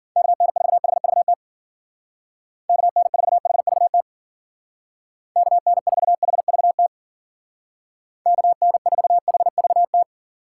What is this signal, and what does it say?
Morse code, heard as a keyed tone, about 49 wpm, 700 Hz. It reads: KN4HVT KN4HVT KN4HVT KN4HVT
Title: Speed Racing - USA 2x3 Call Signs 40wpm